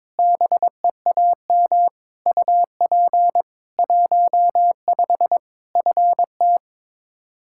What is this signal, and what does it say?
Morse code, heard as a keyed tone, about 21 wpm, 700 Hz. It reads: BEAM UP 15 FT
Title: Practice QSO - Exchange 2 - Info - 22wpm